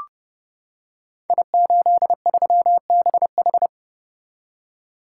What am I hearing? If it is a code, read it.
Morse code, heard as a keyed tone, about 30 wpm, 700 Hz. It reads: I83BH
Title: Code Groups - 5 Letter:Numbers - 30wpm